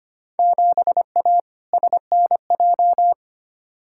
Morse code, heard as a keyed tone, about 25 wpm, 700 Hz. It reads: 7A SNJ
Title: Field Day Exchanges Encore - 25wpm